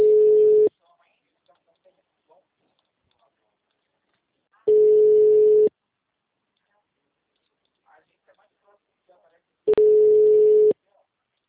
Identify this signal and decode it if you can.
Ringback tone (Europe, 425 Hz), with speech.
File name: out- -2002- - - .52645